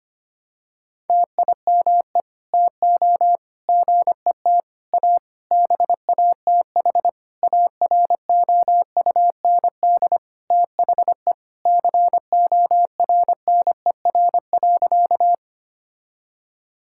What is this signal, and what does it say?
Morse code, heard as a keyed tone, about 25 wpm, 700 Hz. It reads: TIME TO GET A BATH AROUND THE CORNER.
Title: Sentences from Top 1600 Words 25wpm